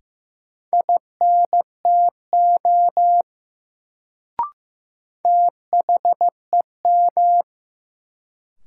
Morse code, heard as a keyed tone, about 15 wpm, 700 Hz. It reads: INTO THEM
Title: Sets of 2 Words - New Word Drill - 15wpm